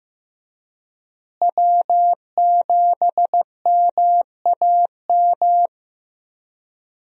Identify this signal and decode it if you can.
Morse code, heard as a keyed tone, about 15 wpm, 700 Hz. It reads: W7MAM
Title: USA Call Signs - 5 Characters - Repeated 3x - 15wpm